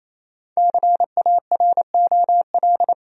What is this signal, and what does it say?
Morse code, heard as a keyed tone, about 28 wpm, 700 Hz. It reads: CAROL
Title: USA Common Names - Max 5 Letters - 28wpm